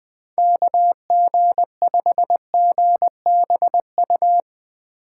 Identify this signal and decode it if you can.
Morse code, heard as a keyed tone, about 21 wpm, 700 Hz. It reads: KG5GBU
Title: USA Call Signs - 3x Character Spacing - 20wpm